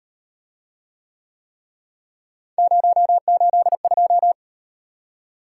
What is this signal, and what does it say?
Morse code, heard as a keyed tone, about 35 wpm, 700 Hz. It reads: 082